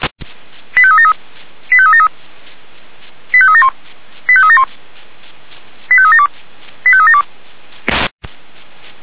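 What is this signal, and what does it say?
Five-tone selective call (ZVEI), heard as quick runs of five tones, tones of about 70 ms. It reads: 85373 85373 85371 85371 85372 85372
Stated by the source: ZVEI five-tone selective call received on an SDR.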